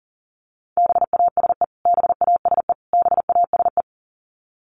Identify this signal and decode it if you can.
Morse code, heard as a keyed tone, about 40 wpm, 700 Hz. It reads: BASE BASE BASE
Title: Top 500 Words Repeated 40wpm